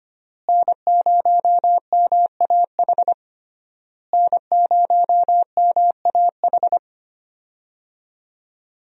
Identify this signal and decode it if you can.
Morse code, heard as a keyed tone, about 25 wpm, 700 Hz. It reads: N0MAH N0MAH
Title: USA Call Signs - 5 Characters - Repeated 3x - 25wpm